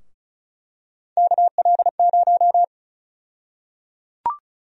Morse code, heard as a keyed tone, about 36 wpm, 700 Hz. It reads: KL0
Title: USA Call Sign Prefixes - 35wpm